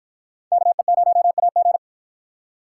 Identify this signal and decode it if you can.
Morse code, heard as a keyed tone, about 53 wpm, 700 Hz. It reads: KE0AG